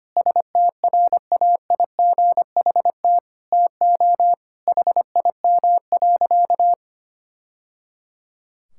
Morse code, heard as a keyed tone, about 25 wpm, 700 Hz. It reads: STRAIGHT TO HIM.
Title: Sentences from Top 1000 Words - Review Part 6 - 25wpm